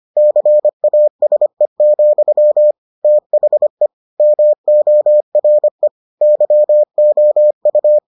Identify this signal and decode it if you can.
Morse code, heard as a keyed tone, about 25 wpm, 600 Hz. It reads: CASE, THE MORE YOU